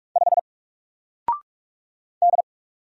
Morse code, heard as a keyed tone, about 44 wpm, 700 Hz. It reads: 5 D
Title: Single Letter-Number 45wpm